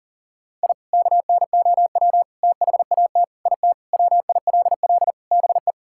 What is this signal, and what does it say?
Morse code, heard as a keyed tone, about 37 wpm, 700 Hz. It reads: I KNOW THAT IT WILL BE
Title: Sentences from Top 1000 Words - Review Part 6 - 40wpm